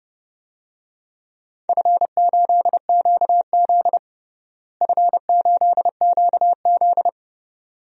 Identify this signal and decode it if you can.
Morse code, heard as a keyed tone, about 30 wpm, 700 Hz. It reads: F8QZ F8QZ
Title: DXCC-100 Call Signs - 4 Characters - Repeated 2x - 30wpm